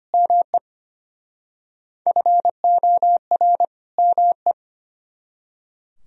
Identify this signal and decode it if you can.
Morse code, heard as a keyed tone, about 31 wpm, 700 Hz. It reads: ME FOR ME
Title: Speed-Racing - Sets of 2 Words 25wpm